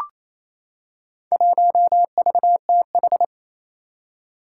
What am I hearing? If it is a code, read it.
Morse code, heard as a keyed tone, about 28 wpm, 700 Hz. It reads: 1VTH
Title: Four Challenging Letter-Number Combinations 28wpm